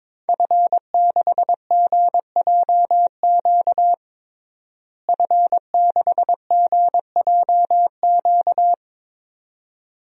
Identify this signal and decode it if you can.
Morse code, heard as a keyed tone, about 22 wpm, 700 Hz. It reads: F6GJQ F6GJQ
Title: DXCC-100 Call Signs - 5 Characters - Repeated 2x - 22wpm